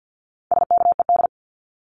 Morse code, heard as a keyed tone, about 50 wpm, 700 Hz. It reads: SKED
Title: Abbreviations 50wpm